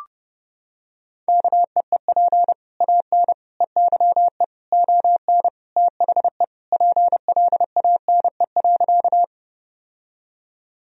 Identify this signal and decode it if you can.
Morse code, heard as a keyed tone, about 32 wpm, 700 Hz. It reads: KEEP AN EYE ON THE PLANE.
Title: Sentences from Top 1000 Words - Review Part 3 - 30wpm